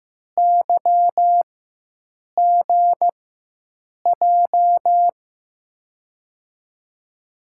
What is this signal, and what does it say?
Morse code, heard as a keyed tone, about 15 wpm, spaced as for 7 wpm, 700 Hz. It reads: YGJ
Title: USA Call Signs - 6 Characters - 4x Character Spacing - 15wpm